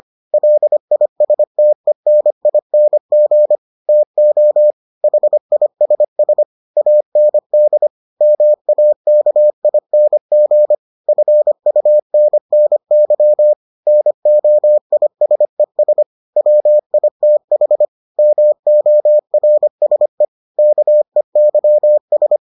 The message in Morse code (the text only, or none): LISTENING TO HISS AND MAKING FUNNY NOISES WITH MORSE KEYS